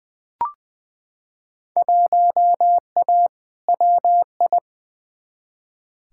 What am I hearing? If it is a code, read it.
Morse code, heard as a keyed tone, about 21 wpm, 700 Hz. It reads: 1A WI
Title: Field Day Exchanges - 20wpm